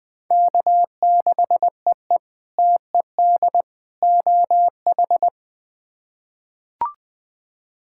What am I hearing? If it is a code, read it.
Morse code, heard as a keyed tone, about 20 wpm, 700 Hz. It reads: K6EE TED OH
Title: K1USN SST QSO Practice - 20wpm